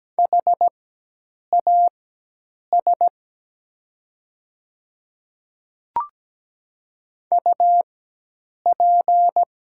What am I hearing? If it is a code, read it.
Morse code, heard as a keyed tone, about 17 wpm, spaced as for 8 wpm, 700 Hz. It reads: HAS UP
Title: 4x Character Spacing - Top 100 Words - 17wpm